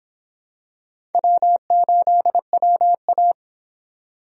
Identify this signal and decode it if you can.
Morse code, heard as a keyed tone, about 26 wpm, 700 Hz. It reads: W8WA